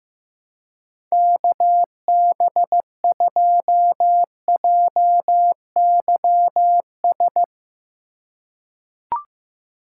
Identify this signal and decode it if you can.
Morse code, heard as a keyed tone, about 15 wpm, 700 Hz. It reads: KB2JYS
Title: USA Call Signs - 6 Characters - 3x Character Spacing - 15wpm